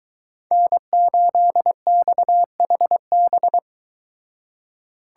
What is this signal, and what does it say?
Morse code, heard as a keyed tone, about 23 wpm, 700 Hz. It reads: N8XHB